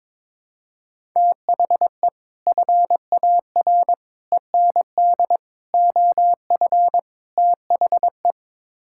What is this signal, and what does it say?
Morse code, heard as a keyed tone, about 22 wpm, 700 Hz. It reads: THE FAR END OF THE
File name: practice / 22wpm / Build A Phrase - 5-Word Phrases - 22wpm